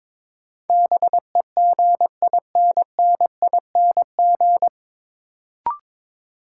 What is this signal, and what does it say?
Morse code, heard as a keyed tone, about 22 wpm, 700 Hz. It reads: BEGINNING
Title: Sentences from Top 800 Words - New Word Drill - 22wpm